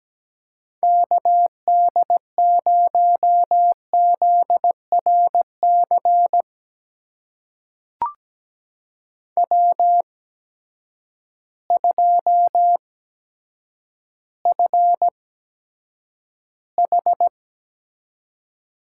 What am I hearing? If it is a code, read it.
Morse code, heard as a keyed tone, about 17 wpm, spaced as for 8 wpm, 700 Hz. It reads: KD0ZRC W2FH
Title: USA Call Signs - 8x Character Spacing - 17wpm